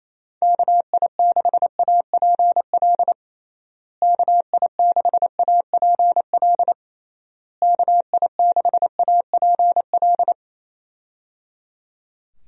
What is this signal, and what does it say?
Morse code, heard as a keyed tone, about 27 wpm, 700 Hz. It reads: KI6APL KI6APL KI6APL
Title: USA Call Signs - 6 Characters - Repeated 3x - 28wpm